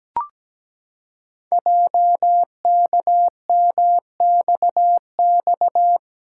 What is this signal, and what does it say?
Morse code, heard as a keyed tone, about 17 wpm, 700 Hz. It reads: JKMXX
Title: Code Groups - 5 Letters - 17wpm